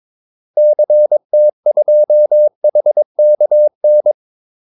Morse code, heard as a keyed tone, about 22 wpm, 600 Hz. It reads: CT2HKN
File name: Callsigns-22wpm